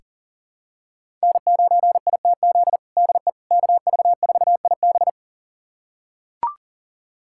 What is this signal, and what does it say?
Morse code, heard as a keyed tone, about 40 wpm, 700 Hz. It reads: N9ITZ DE KV4IB
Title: USA Callsign DE Callsign - 5 Characters - 40wpm